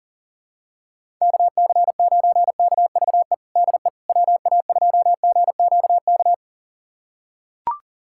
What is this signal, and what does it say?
Morse code, heard as a keyed tone, about 40 wpm, 700 Hz. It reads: KC9KVE DE WA2GQK
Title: USA Callsign DE Callsign - 6 Characters - 40wpm